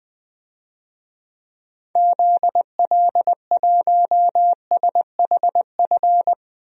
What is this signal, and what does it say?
Morse code, heard as a keyed tone, about 20 wpm, 700 Hz. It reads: ZL1SHF
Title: DXCC-100 Call Signs - 6 Characters - 20wpm